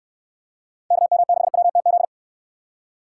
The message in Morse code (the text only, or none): KM6WTZ